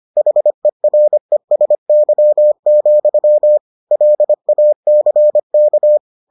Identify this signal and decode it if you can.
Morse code, heard as a keyed tone, about 25 wpm, 600 Hz. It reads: HERESY, LACK